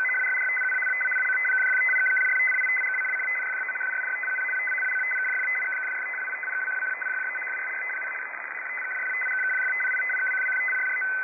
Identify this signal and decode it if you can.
Two-tone teleprinter signal, about 500 Hz shift.